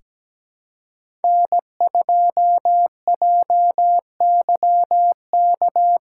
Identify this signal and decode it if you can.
Morse code, heard as a keyed tone, about 17 wpm, 700 Hz. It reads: N2JYK
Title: USA Callsign DE Callsign - 5 Characters - 17wpm